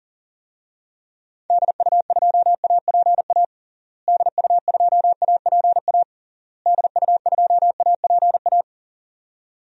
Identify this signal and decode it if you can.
Morse code, heard as a keyed tone, about 40 wpm, 700 Hz. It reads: DU2APA DU2APA DU2APA
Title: DXCC-100 Call Signs - 6 Characters - Repeated 3x - 40wpm